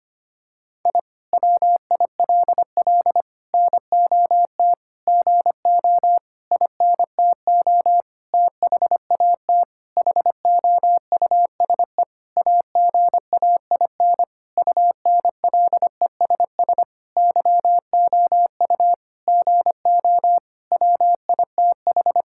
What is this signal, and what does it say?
Morse code, heard as a keyed tone, about 25 wpm, 700 Hz. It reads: I WILL NOT GO INTO THAT HOUSE AGAIN UNLESS YOU GO WITH
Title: Sentences from Top 1000 Words - Review Part 6 - 25wpm